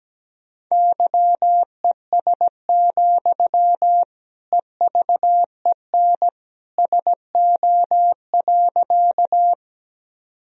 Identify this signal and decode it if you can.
Morse code, heard as a keyed tone, about 17 wpm, 700 Hz. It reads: YES, EVEN SO.
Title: Sentences from Top 1000 Words - Review Part 1 - 17wpm